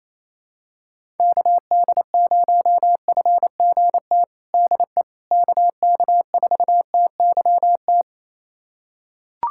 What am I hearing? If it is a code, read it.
Morse code, heard as a keyed tone, about 28 wpm, 700 Hz. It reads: KD0FGT DE KK4TYT
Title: USA Callsign DE Callsign - 6 Characters - 28wpm